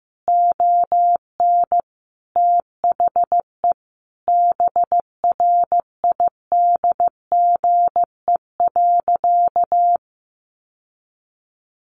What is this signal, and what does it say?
Morse code, heard as a keyed tone, about 15 wpm, 700 Hz. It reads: ON THE BRIDGE.
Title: Sentences from Top 1200 Words 15wpm